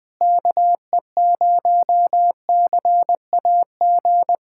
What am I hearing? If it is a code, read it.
Morse code, heard as a keyed tone, about 21 wpm, 700 Hz. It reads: KE0CAG
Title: USA Callsign DE Callsign 20wpm